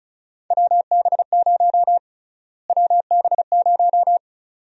Morse code, heard as a keyed tone, about 36 wpm, 700 Hz. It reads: WB0 WB0
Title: USA Call Sign Prefixes - Repeated 2x - 35wpm